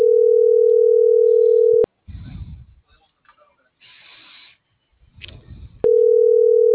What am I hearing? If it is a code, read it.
Ringback tone (North America), 440 + 480 Hz.